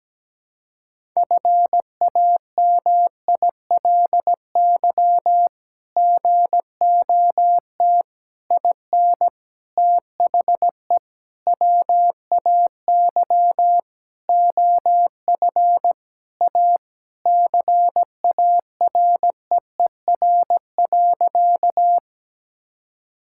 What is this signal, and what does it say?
Morse code, heard as a keyed tone, about 17 wpm, 700 Hz. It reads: FAMILY GOT IN THE WAY OF A CAREER.
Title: Sentences from Top 1000 Words - Review Part 6 - 17wpm